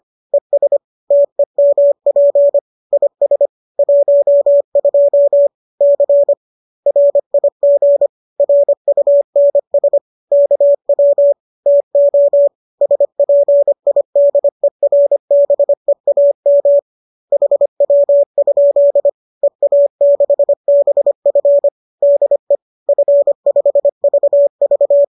Morse code, heard as a keyed tone, about 25 wpm, 570 Hz. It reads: ES TEMP IS 12 C RIG RUNS KW TO SPIDERBEAM HW? EA6BF DE F5VV